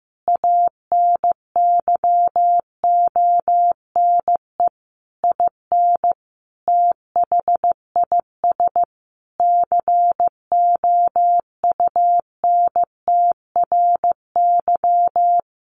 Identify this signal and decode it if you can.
Morse code, heard as a keyed tone, about 15 wpm, 700 Hz. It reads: ANYONE IN THIS COUNTRY
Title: Sentences from Top 1200 Words 15wpm